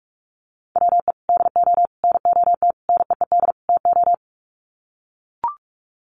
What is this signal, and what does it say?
Morse code, heard as a keyed tone, about 44 wpm, 700 Hz. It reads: WE DO NOT NEED TO
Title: Sets of 5 Words Encore 45wpm